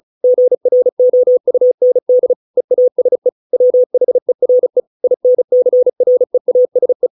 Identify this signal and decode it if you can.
Morse code, heard as a keyed tone, about 34 wpm, 500 Hz. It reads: GROUND EASE WHERE INCREASE